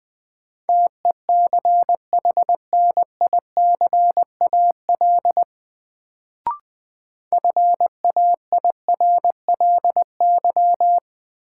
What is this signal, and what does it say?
Morse code, heard as a keyed tone, about 20 wpm, 700 Hz. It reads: TECHNICAL FAIRLY
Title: Sentences from Top 1100 Words - New Word Drill Alt - 20wpm